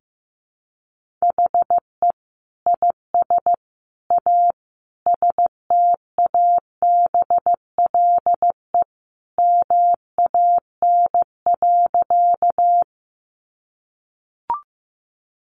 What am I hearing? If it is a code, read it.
Morse code, heard as a keyed tone, about 15 wpm, 700 Hz. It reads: HE IS A STABLE MAN.
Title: Sentences from Top 1500 Words 15wpm